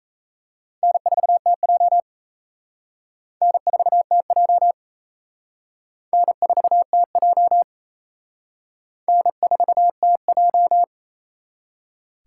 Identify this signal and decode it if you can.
Morse code, heard as a keyed tone, about 34 wpm, 700 Hz. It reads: N4TJ N4TJ N4TJ N4TJ
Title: Speed Racing - USA Call Signs 28wpm